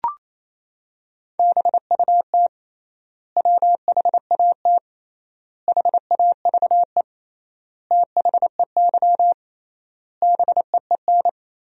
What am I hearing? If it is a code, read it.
Morse code, heard as a keyed tone, about 28 wpm, 700 Hz. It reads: BUT WHAT HAVE THEY BEEN
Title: 3x Word-Spacing - Sentences from Top 100 Words - 28wpm